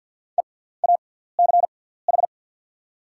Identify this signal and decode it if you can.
Morse code, heard as a keyed tone, about 48 wpm, spaced as for 17 wpm, 700 Hz. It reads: EACH
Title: 6x Character Spacing - Top 100 Words - 50wpm